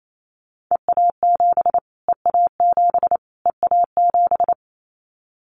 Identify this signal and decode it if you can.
Morse code, heard as a keyed tone, about 28 wpm, 700 Hz. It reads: EA7 EA7 EA7
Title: Callsign Prefixes for Top 25 DXCC - Repeated 3x - 28wpm